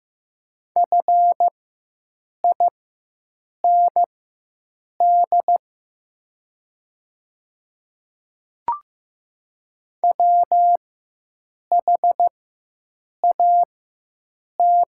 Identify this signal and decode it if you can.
Morse code, heard as a keyed tone, about 15 wpm, spaced as for 7 wpm, 700 Hz. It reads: FIND WHAT
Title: 4x Character Spacing - Top 200 Words - 15wpm